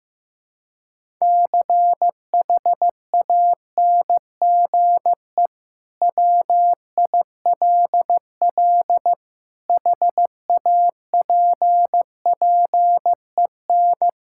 Morse code, heard as a keyed tone, about 15 wpm, 700 Hz. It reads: CHANGE WILL HAPPEN
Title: Sentences from Top 1000 Words - Review Part 5 - 15wpm